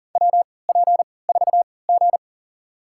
Morse code, heard as a keyed tone, about 40 wpm, spaced as for 23 wpm, 700 Hz. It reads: WP4G
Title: USA Call Signs - 4 Characters - 3x Character Spacing - 40wpm